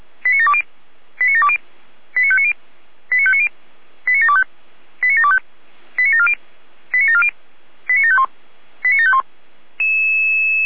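Five-tone selective call (ZVEI), heard as quick runs of five tones, tones of about 70 ms. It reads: 89720 89720 89590 89590 89725 89725 89740 89740 89751 89751
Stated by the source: ZVEI five-tone selective call received on an SDR.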